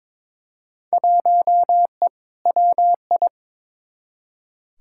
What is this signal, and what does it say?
Morse code, heard as a keyed tone, about 22 wpm, 700 Hz. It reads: 1E WI
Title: Field Day Exchanges - 22wpm